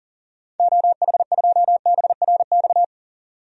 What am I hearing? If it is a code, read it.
Morse code, heard as a keyed tone, about 40 wpm, 700 Hz. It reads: OH2BRX
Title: DXCC-100 Call Signs - 6 Characters - 40wpm